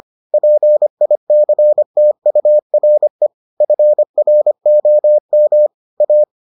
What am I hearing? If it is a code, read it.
Morse code, heard as a keyed tone, about 25 wpm, 600 Hz. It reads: PICTURE FROM A